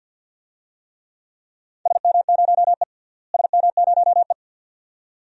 Morse code, heard as a keyed tone, about 46 wpm, 700 Hz. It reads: SM0E SM0E
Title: DXCC-100 Call Signs - 4 Characters - Repeated 2x - 50wpm